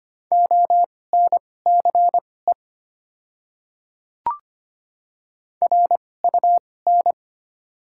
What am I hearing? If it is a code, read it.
Morse code, heard as a keyed tone, about 25 wpm, spaced as for 18 wpm, 690 Hz. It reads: ONCE RUN
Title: 2x Character Spacing - Top 300 Words - 25wpm